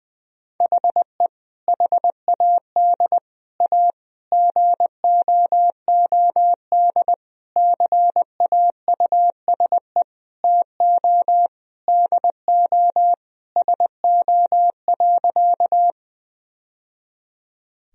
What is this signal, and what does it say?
Morse code, heard as a keyed tone, about 20 wpm, 700 Hz. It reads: HE HAD A GOOD CAUSE TO DO SO.
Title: Sentences from Top 1000 Words - Review Part 1 - 20wpm